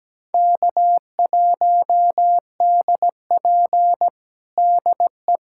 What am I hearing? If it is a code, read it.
Morse code, heard as a keyed tone, about 17 wpm, 700 Hz. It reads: K1DP DE
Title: USA Callsign DE Callsign - 4 Characters - 17wpm